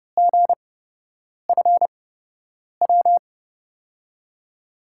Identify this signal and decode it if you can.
Morse code, heard as a keyed tone, about 31 wpm, spaced as for 8 wpm, 700 Hz. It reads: GFW
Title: USA Call Signs - 5 Characters - 8x Character Spacing - 30wpm